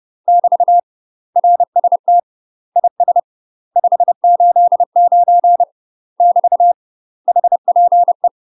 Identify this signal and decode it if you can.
Morse code, heard as a keyed tone, about 30 wpm, 700 Hz. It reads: <BT> RST IS 589 <BT> HPE
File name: UK MASTER QSOs TRAINING / 30WPM / UK MASTER QSOs 0013